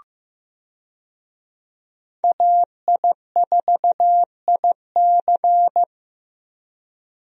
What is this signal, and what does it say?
Morse code, heard as a keyed tone, about 15 wpm, 700 Hz. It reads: AI4IC
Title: USA Call Signs - 5 Characters - 15wpm